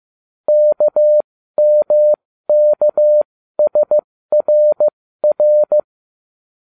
Morse code, heard as a keyed tone, about 15 wpm, spaced as for 13 wpm, 600 Hz. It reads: KMKSRR